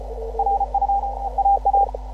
Morse code, heard as a keyed tone, about 31 wpm, 750 Hz. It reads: RU AA